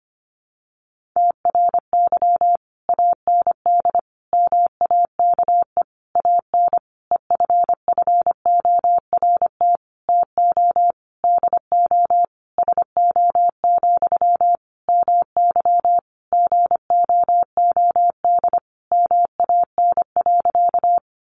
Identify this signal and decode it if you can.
Morse code, heard as a keyed tone, about 25 wpm, 700 Hz. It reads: TRY AND MAKE AN EFFORT TO DO SO, MY GOOD MAN.
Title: Sentences from Top 1200 Words 25wpm